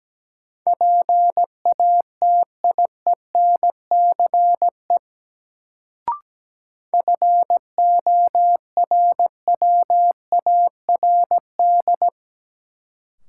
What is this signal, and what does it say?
Morse code, heard as a keyed tone, about 17 wpm, 700 Hz. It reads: PATIENCE FORWARD
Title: Sentences from Top 1300 Words - New Word Drill - 17wpm